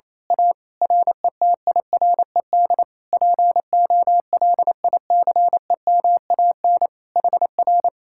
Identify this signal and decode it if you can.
Morse code, heard as a keyed tone, about 27 wpm, 700 Hz. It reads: A RETIRED POLICEMAN HR
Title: Speed-Racing Common QSO Phrases - 28wpm